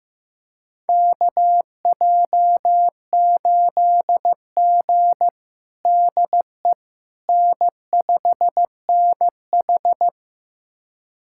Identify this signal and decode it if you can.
Morse code, heard as a keyed tone, about 15 wpm, 700 Hz. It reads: KJ8G DE N5NH
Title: USA Callsign DE Callsign - 4 Characters - 15wpm